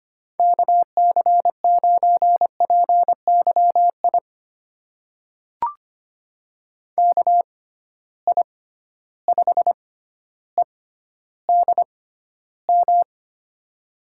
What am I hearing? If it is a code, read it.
Morse code, heard as a keyed tone, about 25 wpm, spaced as for 13 wpm, 700 Hz. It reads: KC9PYI KI5EDM
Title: USA Call Signs - 6 Characters - 6x Character Spacing - 25wpm